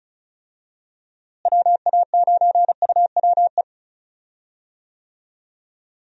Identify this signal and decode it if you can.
Morse code, heard as a keyed tone, about 34 wpm, 700 Hz. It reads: WA9UWE
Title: USA Call Signs - 6 Characters - Repeated 3x - 35wpm